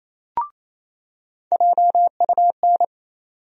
Morse code, heard as a keyed tone, about 28 wpm, 700 Hz. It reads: JUN
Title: USA Call Sign Postfixes - 3 Letters 28wpm